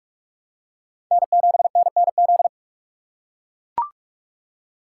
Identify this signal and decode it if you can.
Morse code, heard as a keyed tone, about 42 wpm, 700 Hz. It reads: N7NNZ
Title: USA Call Signs - 6x Character Spacing - 45wpm